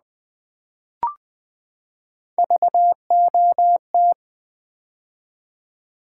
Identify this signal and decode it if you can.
Morse code, heard as a keyed tone, about 20 wpm, 700 Hz. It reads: VOT
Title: Build A Word - 3 Letters At A Time - 6-Letter Words - 20wpm